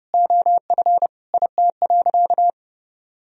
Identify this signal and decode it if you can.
Morse code, heard as a keyed tone, about 32 wpm, 700 Hz. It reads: OF IT.
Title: Sentences from Top 1000 Words - Review Part 2 - 30wpm